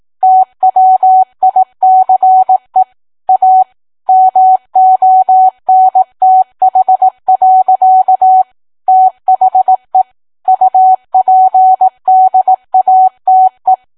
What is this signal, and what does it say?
Morse code, heard as a keyed tone, about 18 wpm, 750 Hz. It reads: TWICE A MONTH. THE UPDATE